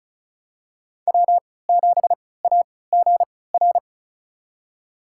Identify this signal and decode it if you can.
Morse code, heard as a keyed tone, about 36 wpm, spaced as for 20 wpm, 700 Hz. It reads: W7AGR
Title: USA Call Signs - 5 Characters - 3x Character Spacing - 35wpm